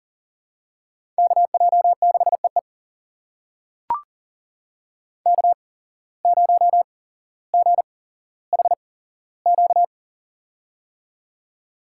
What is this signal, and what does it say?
Morse code, heard as a keyed tone, about 42 wpm, spaced as for 17 wpm, 700 Hz. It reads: KJ6EE K0GHQ
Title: USA Call Signs - 5 Characters - 8x Character Spacing - 40wpm